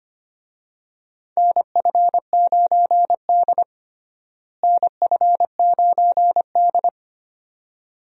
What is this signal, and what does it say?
Morse code, heard as a keyed tone, about 25 wpm, 700 Hz. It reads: NF9D NF9D
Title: USA Call Signs - 4 Characters - Repeated 2x - 25wpm